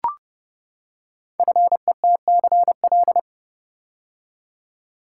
Morse code, heard as a keyed tone, about 30 wpm, 700 Hz. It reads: FETCL
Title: Code Groups - 5 Letters - 30wpm